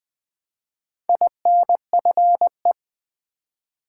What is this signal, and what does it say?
Morse code, heard as a keyed tone, about 20 wpm, 700 Hz. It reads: INFE
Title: Build A Word - 1 Letter At A Time - 5-Letter Words - 20wpm